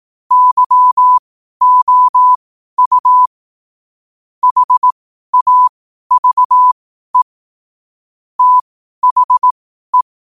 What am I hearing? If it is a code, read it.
Morse code, heard as a keyed tone, about 18 wpm, spaced as for 12 wpm, 1000 Hz. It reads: YOU HAVE THE
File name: qotdfiles / 12wpm